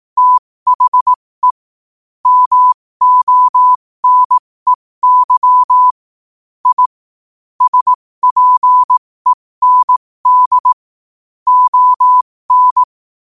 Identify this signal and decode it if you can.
Morse code, heard as a keyed tone, about 18 wpm, spaced as for 15 wpm, 1000 Hz. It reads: THE MONEY I SPEND ON
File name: qotdfiles / 15wpm